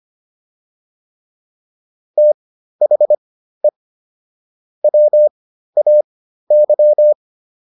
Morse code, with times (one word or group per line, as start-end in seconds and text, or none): 2.2-3.7 THE
4.8-7.1 WAY